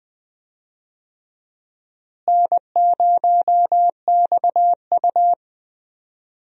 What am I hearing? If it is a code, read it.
Morse code, heard as a keyed tone, about 20 wpm, 700 Hz. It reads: N0XU
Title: USA Call Signs - 4 Characters - Repeated 2x - 20wpm